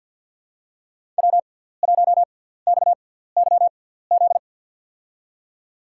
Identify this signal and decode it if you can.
Morse code, heard as a keyed tone, about 46 wpm, spaced as for 17 wpm, 700 Hz. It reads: W1XYZ
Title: USA Call Signs - 5 Characters - 6x Character Spacing - 50wpm